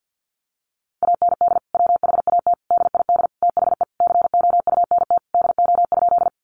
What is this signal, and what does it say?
Morse code, heard as a keyed tone, about 50 wpm, 700 Hz. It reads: AND WHAT DID THE COUNT DO?
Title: Sentences from Top 1500 Words 50wpm